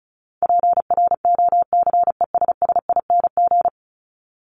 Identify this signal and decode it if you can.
Morse code, heard as a keyed tone, about 34 wpm, 700 Hz. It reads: PROCESSING
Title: ING Suffix - Top 100 Words with Base Words 35wpm